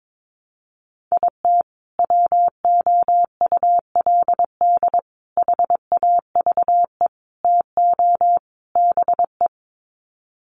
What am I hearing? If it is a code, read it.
Morse code, heard as a keyed tone, about 22 wpm, 700 Hz. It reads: IT WOULD HAVE TO BE
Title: Sets of 5 Words Encore 22wpm